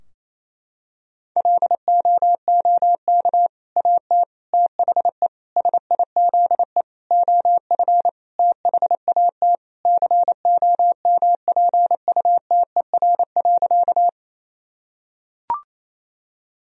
Morse code, heard as a keyed tone, about 28 wpm, 700 Hz. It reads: LOOK AT THE SIZE OF THAT COMPUTER.
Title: Sentences from Top 1000 Words - Review Part 3 - 28wpm